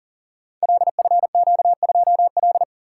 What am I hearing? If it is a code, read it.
Morse code, heard as a keyed tone, about 40 wpm, 700 Hz. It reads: LFQ2L
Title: Code Groups - 5 Letter:Numbers - 40wpm